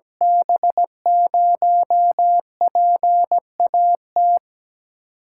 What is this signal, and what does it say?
Morse code, heard as a keyed tone, about 17 wpm, 700 Hz. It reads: B0PAT
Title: USA Call Signs - 8x Character Spacing - 17wpm